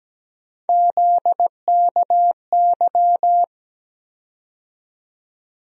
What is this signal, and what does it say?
Morse code, heard as a keyed tone, about 17 wpm, 700 Hz. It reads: ZKY
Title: USA Call Sign Postfixes - 3 Letters 17wpm